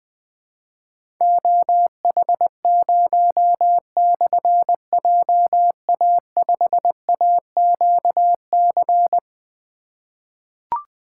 Morse code, heard as a keyed tone, about 21 wpm, 700 Hz. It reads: OH0/JA5AQC
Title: Difficult Call Signs 20wpm